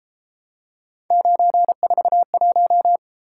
Morse code, heard as a keyed tone, about 32 wpm, 700 Hz. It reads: 941